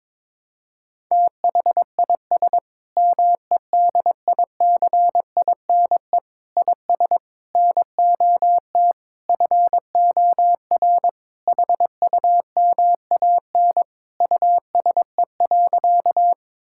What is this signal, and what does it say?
Morse code, heard as a keyed tone, about 22 wpm, 700 Hz. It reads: THIS MEDICINE IS NOT FOR HUMAN USE.
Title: Sentences from Top 1000 Words - Review Part 6 - 22wpm